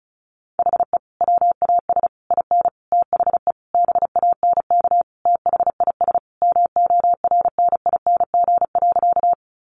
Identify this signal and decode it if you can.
Morse code, heard as a keyed tone, about 34 wpm, 700 Hz. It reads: HE WAS IN THE BANK THIS MORNING.
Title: Sentences from Top 900 Words 35wpm